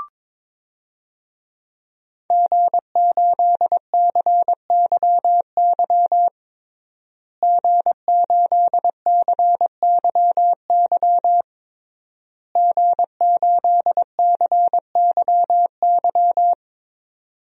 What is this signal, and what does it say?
Morse code, heard as a keyed tone, about 22 wpm, 700 Hz. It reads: G8CYY G8CYY G8CYY
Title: DXCC-100 Call Signs - 5 Characters - Repeated 3x - 22wpm